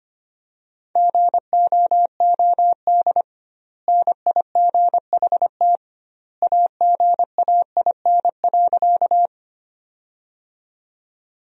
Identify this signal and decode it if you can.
Morse code, heard as a keyed tone, about 25 wpm, 700 Hz. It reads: GOOD NIGHT AGAIN.
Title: 2x Word-Spacing - Sentences from Top 300 Words - 25wpm